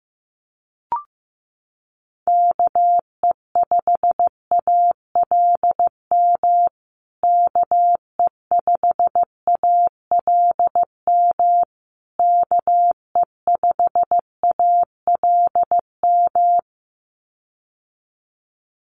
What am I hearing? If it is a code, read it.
Morse code, heard as a keyed tone, about 15 wpm, 700 Hz. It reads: KE5ALM KE5ALM KE5ALM
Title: USA Call Signs Repeated 3x 15wpm